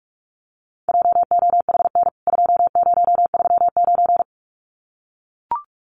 Morse code, heard as a keyed tone, about 44 wpm, 700 Hz. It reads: JOHN 2039
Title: CWT Exchange 45wpm